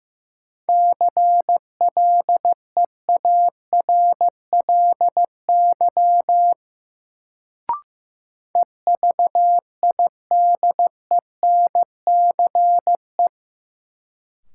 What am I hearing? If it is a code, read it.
Morse code, heard as a keyed tone, about 16 wpm, 700 Hz. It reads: CLEARLY EVIDENCE
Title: Sentences from Top 1300 Words - New Word Drill Alt - 15wpm